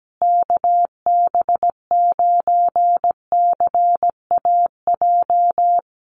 Morse code, heard as a keyed tone, about 17 wpm, 700 Hz. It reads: KB9CAJ
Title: USA Call Signs Repeated 2x 17wpm